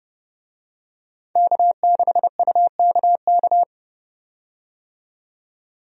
Morse code, heard as a keyed tone, about 30 wpm, 700 Hz. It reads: K6UKK
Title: USA Call Signs - 5 Characters - 30wpm